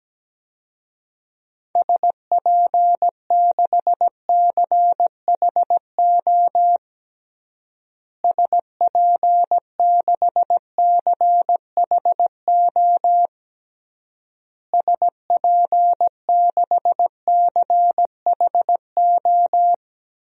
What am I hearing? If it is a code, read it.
Morse code, heard as a keyed tone, about 17 wpm, 700 Hz. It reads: SP6CHO SP6CHO SP6CHO
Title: DXCC-100 Call Signs - 6 Characters - Repeated 3x - 17wpm